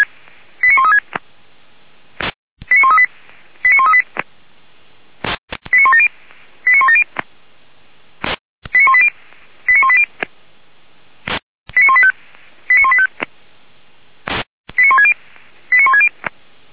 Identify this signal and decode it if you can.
Five-tone selective call (ZVEI), heard as quick runs of five tones, tones of about 70 ms. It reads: 89136 89138 89138 89170 89170 89180 89180 89185 89185 89160 89160